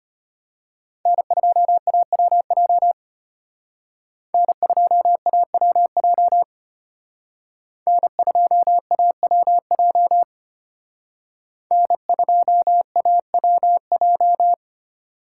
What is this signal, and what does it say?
Morse code, heard as a keyed tone, about 33 wpm, 700 Hz. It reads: N2AWJ N2AWJ N2AWJ N2AWJ